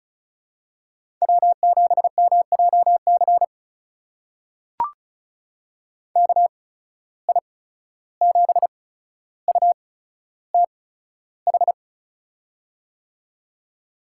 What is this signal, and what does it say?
Morse code, heard as a keyed tone, about 36 wpm, spaced as for 14 wpm, 700 Hz. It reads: W7MJC KI7UTH
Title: USA Call Signs - 8x Character Spacing - 35wpm